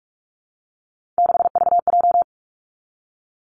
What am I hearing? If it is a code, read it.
Morse code, heard as a keyed tone, about 46 wpm, 700 Hz. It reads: 6VJ
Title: Three Challenging Letter-Number Combinations 45wpm